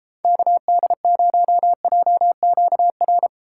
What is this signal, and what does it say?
Morse code, heard as a keyed tone, about 33 wpm, 700 Hz. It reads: KD0JQR